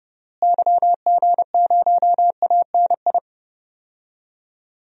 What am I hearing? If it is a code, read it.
Morse code, heard as a keyed tone, about 32 wpm, 700 Hz. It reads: YG0ANI
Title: DXCC-100 Call Signs - 6 Characters - Repeated 3x - 30wpm